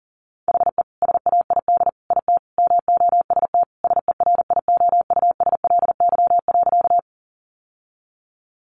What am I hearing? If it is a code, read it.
Morse code, heard as a keyed tone, about 40 wpm, 700 Hz. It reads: HE SAID IT MOST SERIOUSLY.
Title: Sentences from Top 1200 Words 40wpm